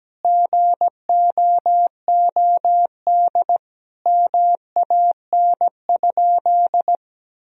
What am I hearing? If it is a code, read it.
Morse code, heard as a keyed tone, about 17 wpm, 700 Hz. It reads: GOOD MAN?
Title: Sentences from Top 1000 Words - Review Part 1 - 17wpm